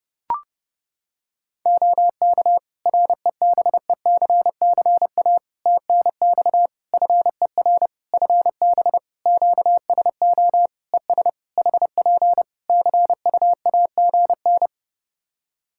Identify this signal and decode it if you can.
Morse code, heard as a keyed tone, about 30 wpm, 700 Hz. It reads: OK REBECCA TNX FER FB QSO ES HP CUAGN
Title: Practice QSO - Exchange 4 - Ending - 30wpm